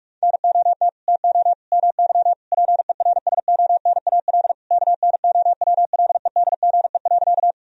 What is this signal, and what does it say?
Morse code, heard as a keyed tone, about 45 wpm, 700 Hz. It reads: NOT TO MY PERSONAL KNOWLEDGE.
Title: Sentences from Top 1000 Words - Review Part 4 - 45wpm